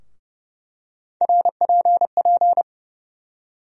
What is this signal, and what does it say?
Morse code, heard as a keyed tone, about 30 wpm, 700 Hz. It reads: RPP
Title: USA Call Sign Postfixes - 3 Letters 30wpm